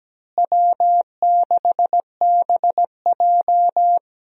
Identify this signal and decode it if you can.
Morse code, heard as a keyed tone, about 17 wpm, 700 Hz. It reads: W6BJ
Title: USA Call Signs - 4 Characters - Repeated 2x - 17wpm